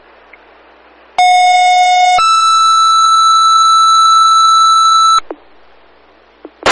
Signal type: two-tone page